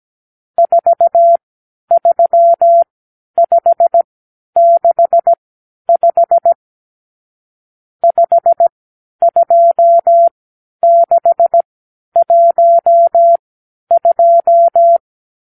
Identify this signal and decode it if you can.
Morse code, heard as a keyed tone, about 17 wpm, spaced as for 10 wpm, 680 Hz. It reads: 43565 52612
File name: k7qo / Lesson 52 (8)